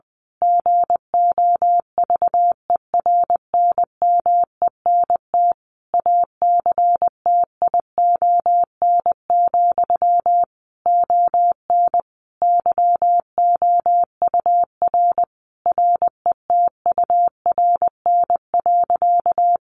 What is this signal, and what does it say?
Morse code, heard as a keyed tone, about 20 wpm, 700 Hz. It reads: GOVERNMENT ACTION, ON YOUR RETURN.
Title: A Sound Of Thunder 20wpm